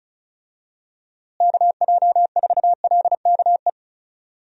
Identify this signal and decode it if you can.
Morse code, heard as a keyed tone, about 36 wpm, 700 Hz. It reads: KJ4LKE
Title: USA Call Signs - 6 Characters - Repeated 2x - 35wpm